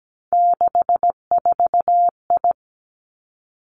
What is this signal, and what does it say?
Morse code, heard as a keyed tone, about 17 wpm, 700 Hz. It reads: 64I
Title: Three Challenging Letter-Number Combinations 17wpm